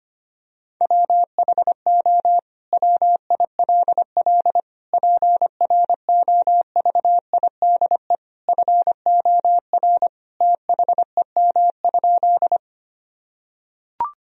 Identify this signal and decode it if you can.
Morse code, heard as a keyed tone, about 25 wpm, 700 Hz. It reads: WHO WILL PROVIDE FOR THEM?
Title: Sentences from Top 1000 Words - Review Part 3 - 25wpm